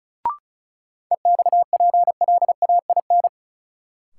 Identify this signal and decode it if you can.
Morse code, heard as a keyed tone, about 34 wpm, 700 Hz. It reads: EXPLAIN
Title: Sentences from Top 1100 Words - New Word Drill - 35wpm